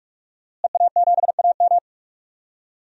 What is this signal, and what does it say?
Morse code, heard as a keyed tone, about 46 wpm, 700 Hz. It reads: EA7AM
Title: DXCC-100 Call Signs - 5 Characters - Repeated 2x - 45wpm